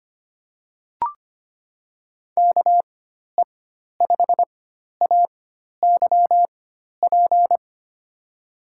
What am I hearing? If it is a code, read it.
Morse code, heard as a keyed tone, about 25 wpm, spaced as for 12 wpm, 700 Hz. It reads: KE5AYP
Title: USA Call Signs - 6 Characters - 4x Character Spacing - 25wpm